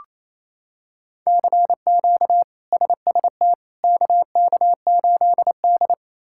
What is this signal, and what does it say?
Morse code, heard as a keyed tone, about 28 wpm, 700 Hz. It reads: CQ SST KK8D
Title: K1USN SST QSO Practice - 28wpm